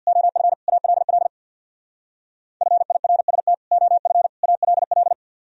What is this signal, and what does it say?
Morse code, heard as a keyed tone, about 60 wpm, 700 Hz. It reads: OF ALL FIRST OF ALL
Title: Speed-Racing - Sets of 3 Words 50wpm